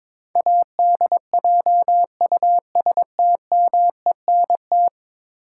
Morse code, heard as a keyed tone, about 21 wpm, 700 Hz. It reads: ADJUSTMENT
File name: practice / 22wpm / 10 Letter Words - First 3 Letters Given - 22wpm